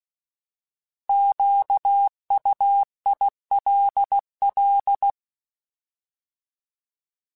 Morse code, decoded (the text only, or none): QUILL